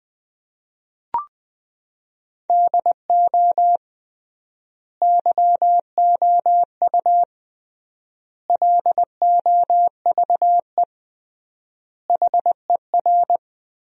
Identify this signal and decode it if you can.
Morse code, heard as a keyed tone, about 20 wpm, 700 Hz. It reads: DO YOU LOVE HER
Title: 3x Word-Spacing - Sentences from Top 400 Words - 20wpm